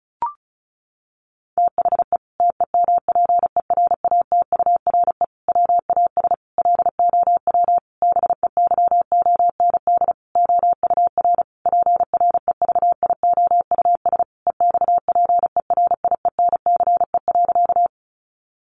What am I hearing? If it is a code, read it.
Morse code, heard as a keyed tone, about 34 wpm, 700 Hz. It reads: THE TEMPERATURE WAS LOW BEYOND OUR PREVIOUS EXPERIENCE.
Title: Sentences from Top 1500 Words 35wpm